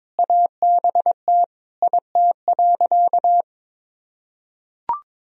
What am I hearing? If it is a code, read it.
Morse code, heard as a keyed tone, about 22 wpm, 700 Hz. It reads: ABT IT.
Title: Sentences from Top 200 Words and Abbreviations 22wpm